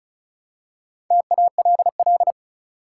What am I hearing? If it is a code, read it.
Morse code, heard as a keyed tone, about 36 wpm, 700 Hz. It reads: TALL
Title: Four Letter Words - 35wpm